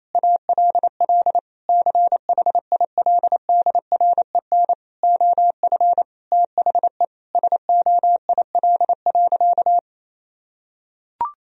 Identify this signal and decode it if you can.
Morse code, heard as a keyed tone, about 27 wpm, 700 Hz. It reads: ALL CHILDREN OF THE SOIL.
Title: Sentences from Top 1000 Words - Review Part 4 - 28wpm